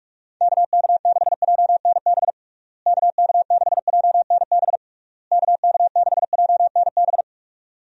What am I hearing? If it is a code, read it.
Morse code, heard as a keyed tone, about 46 wpm, 700 Hz. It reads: KK6JNB KK6JNB KK6JNB
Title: USA Call Signs - 6 Characters - Repeated 3x - 45wpm